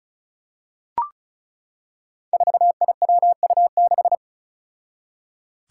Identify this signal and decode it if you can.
Morse code, heard as a keyed tone, about 36 wpm, 700 Hz. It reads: 4IWU6
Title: Five Challenging Letter-Number Combinations 35wpm